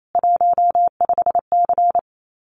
Morse code, heard as a keyed tone, about 28 wpm, 700 Hz. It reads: 15C
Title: Common QSO Phrases 28wpm